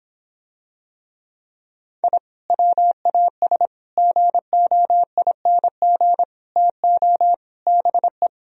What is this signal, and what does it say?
Morse code, heard as a keyed tone, about 26 wpm, 700 Hz. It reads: I WAS GOING TO BE